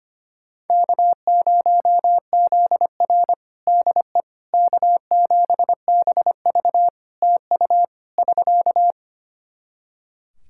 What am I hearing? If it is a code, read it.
Morse code, heard as a keyed tone, about 25 wpm, 700 Hz. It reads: K0ZR DE K7BV TU <SK>
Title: Practice QSO - Exchange 4 - Ending - 25wpm